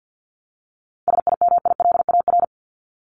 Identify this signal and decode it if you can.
Morse code, heard as a keyed tone, about 50 wpm, 700 Hz. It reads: SIMILAR
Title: Top 501-750 Words 50wpm